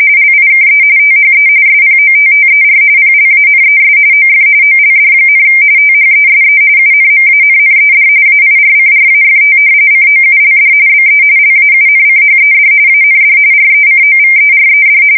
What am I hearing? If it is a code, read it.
RTTY, about 45.45 baud. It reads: LIKE THE SPOTTER PLANES OF OLD, THESE UNMANNED AIRCRAFT COULD BE LAUNCHED AND RECOVERED
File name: fsk170 / battleshipnj